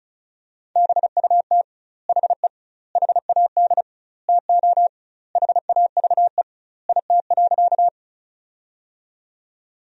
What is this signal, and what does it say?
Morse code, heard as a keyed tone, about 35 wpm, 700 Hz. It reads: BUT HE HAD TO HAVE IT.
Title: 2x Word-Spacing - Sentences from Top 100 Words - 35wpm